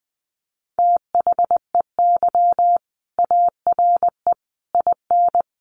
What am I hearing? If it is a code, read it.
Morse code, heard as a keyed tone, about 20 wpm, 700 Hz. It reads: THEY ARE IN
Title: Sentences from Top 1300 Words 20wpm